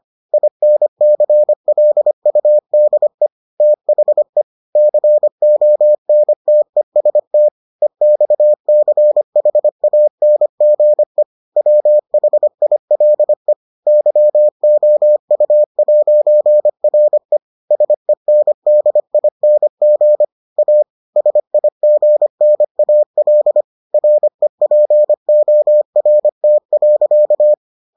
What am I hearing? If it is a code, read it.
Morse code, heard as a keyed tone, about 25 wpm, 600 Hz. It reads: INCLUDE THE CONTEST EXCHANGE WHILE YOU'RE SENDING A SIGNAL REPORT.